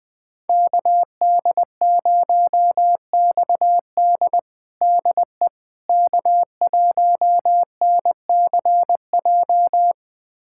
Morse code, heard as a keyed tone, about 20 wpm, 700 Hz. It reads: KD0XD DE K1NCJ
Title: USA Callsign DE Callsign - 5 Characters - 20wpm